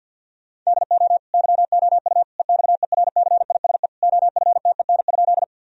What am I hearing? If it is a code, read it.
Morse code, heard as a keyed tone, about 50 wpm, 700 Hz. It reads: DO YOU EXERCISE OFTEN?
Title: Sentences from Top 1000 Words - Review Part 4 - 50wpm